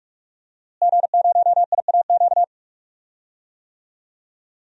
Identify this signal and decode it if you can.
Morse code, heard as a keyed tone, about 46 wpm, 700 Hz. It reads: G0IAQ
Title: DXCC-100 Call Signs - 5 Characters - Repeated 3x - 45wpm